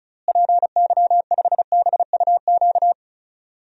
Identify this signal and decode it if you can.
Morse code, heard as a keyed tone, about 36 wpm, 700 Hz. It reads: PY5BUQ
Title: DXCC-100 Call Signs - 6 Characters - Repeated 3x - 35wpm